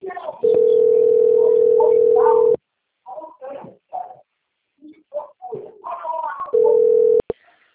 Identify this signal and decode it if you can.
Ringback tone (North America), 440 + 480 Hz, with speech.
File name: out- - - - - .68781